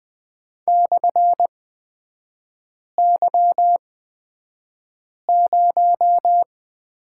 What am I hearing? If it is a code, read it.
Morse code, heard as a keyed tone, about 20 wpm, spaced as for 5 wpm, 700 Hz. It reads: /Y0
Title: 255 Lesson 41e - Period - Three Characters 20wpm